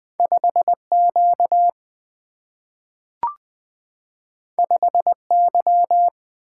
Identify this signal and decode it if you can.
Morse code, heard as a keyed tone, about 20 wpm, 700 Hz. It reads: 5Q 5Y
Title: Single-Letter Single-Number 20wpm